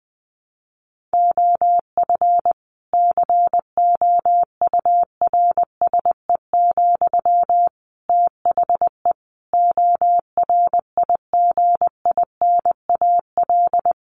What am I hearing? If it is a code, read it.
Morse code, heard as a keyed tone, about 19 wpm, 700 Hz. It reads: OF COURSE, THE ORIGINAL
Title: Sentences from Top 1500 Words 20wpm